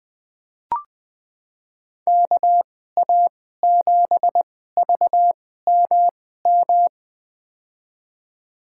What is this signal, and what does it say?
Morse code, heard as a keyed tone, about 20 wpm, spaced as for 14 wpm, 700 Hz. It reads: KA7VMM
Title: USA Call Signs - 6 Characters - 2x Character Spacing - 20wpm